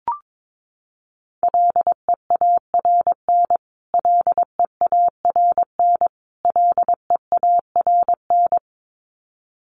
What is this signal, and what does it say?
Morse code, heard as a keyed tone, about 22 wpm, 700 Hz. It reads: LEARN LEARN LEARN
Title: Top 500 Words Repeated 22wpm